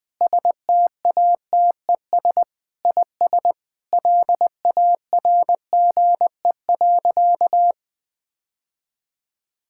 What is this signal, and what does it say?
Morse code, heard as a keyed tone, about 20 wpm, 700 Hz. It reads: STATES IS LARGE.
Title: Sentences with 3-Word Phrases - 20wpm